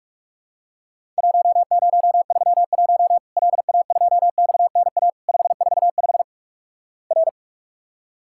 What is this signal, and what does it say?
Morse code, heard as a keyed tone, about 48 wpm, 700 Hz. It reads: 1031 LA2XNA 545 R
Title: Worked All Europe DX Contest - QTC Exchange - 45wpm